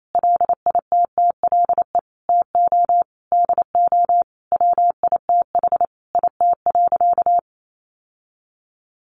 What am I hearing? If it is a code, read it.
Morse code, heard as a keyed tone, about 28 wpm, 700 Hz. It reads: LITTLE TO DO WITH IT.
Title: Sentences from Top 1500 Words 28wpm